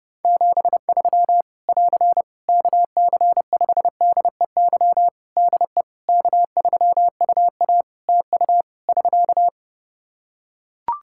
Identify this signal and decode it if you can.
Morse code, heard as a keyed tone, about 31 wpm, 700 Hz. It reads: 73 <AR> KC5DEY DE K3UA TU <SK>
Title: Practice QSO - Exchange 4 - Ending - 30wpm